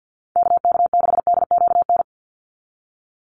Morse code, heard as a keyed tone, about 50 wpm, 700 Hz. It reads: KK6DQN
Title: USA Call Signs 50wpm